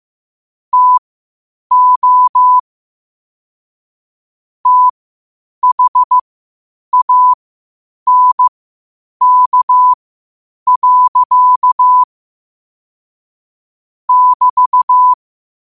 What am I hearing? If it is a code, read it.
Morse code, heard as a keyed tone, about 15 wpm, spaced as for 8 wpm, 1000 Hz. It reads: TO THANK. <BT>
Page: Quotes for Fri, 15 Aug 2025 in Morse Code at 8 words per minute.